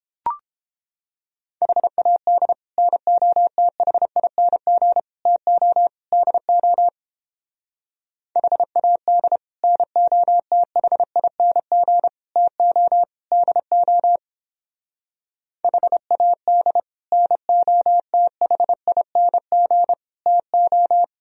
Morse code, heard as a keyed tone, about 29 wpm, 700 Hz. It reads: HAD NOTHING TO DO HAD NOTHING TO DO HAD NOTHING TO